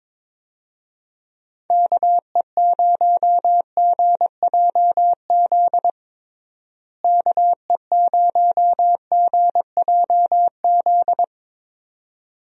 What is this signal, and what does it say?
Morse code, heard as a keyed tone, about 22 wpm, 700 Hz. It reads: KE0GJZ KE0GJZ
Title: USA Call Signs - 6 Characters - Repeated 2x - 22wpm